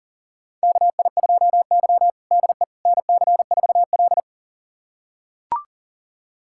Morse code, heard as a keyed tone, about 37 wpm, 700 Hz. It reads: KI2Y DE NC4L
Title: USA Callsign DE Callsign - 4 Characters - 40wpm